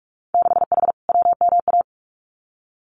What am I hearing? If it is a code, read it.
Morse code, heard as a keyed tone, about 46 wpm, 700 Hz. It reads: 6H WMA
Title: Winter Field Day Exchanges Encore 45wpm